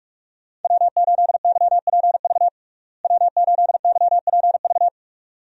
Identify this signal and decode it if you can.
Morse code, heard as a keyed tone, about 46 wpm, 700 Hz. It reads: W8YPV W8YPV
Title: USA Call Signs - 5 Characters - Repeated 2x - 45wpm